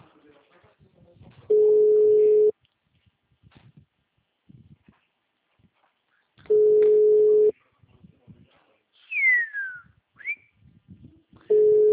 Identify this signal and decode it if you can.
Ringback tone (Europe, 425 Hz), with speech.